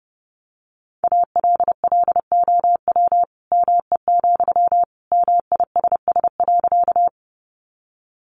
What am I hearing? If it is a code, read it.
Morse code, heard as a keyed tone, about 30 wpm, 700 Hz. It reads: ALLOW ME, MISS.
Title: Sentences from Top 700 Words 30wpm